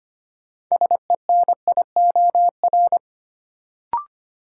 Morse code, heard as a keyed tone, about 25 wpm, 700 Hz. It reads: SENIOR
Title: Sets of 5 Words Encore - New Word Drill - 25wpm